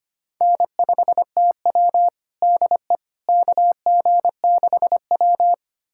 Morse code, heard as a keyed tone, about 25 wpm, 700 Hz. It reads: N5TW DE KG6W
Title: USA Callsign DE Callsign - 4 Characters - 25wpm